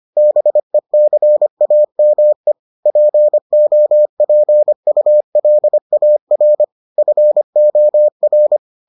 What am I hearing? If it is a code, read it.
Morse code, heard as a keyed tone, about 25 wpm, 600 Hz. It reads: BECAME POPULAR FOR